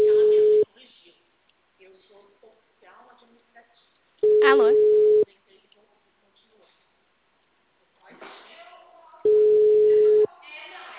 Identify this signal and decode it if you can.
Ringback tone (Europe, 425 Hz), with speech.